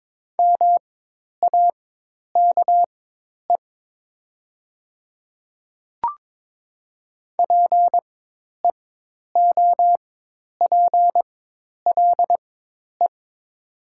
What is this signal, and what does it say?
Morse code, heard as a keyed tone, about 22 wpm, spaced as for 10 wpm, 700 Hz. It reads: MAKE PEOPLE
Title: 4x Character Spacing - Top 200 Words - 22wpm